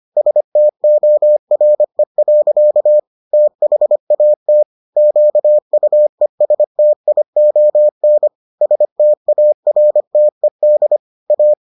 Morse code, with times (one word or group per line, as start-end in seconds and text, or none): 0.2-3.0 STORE.
3.3-4.6 THAT
5.0-8.3 QUESTION
8.6-11.0 STARTED
11.3-11.5 A